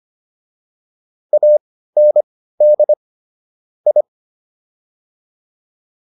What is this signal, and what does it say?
Morse code, heard as a keyed tone, about 25 wpm, spaced as for 15 wpm, 600 Hz. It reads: AND I